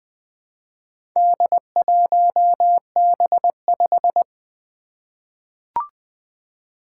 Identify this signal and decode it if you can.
Morse code, heard as a keyed tone, about 20 wpm, 700 Hz. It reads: D1B5
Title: Four Challenging Letter-Number Combinations 20wpm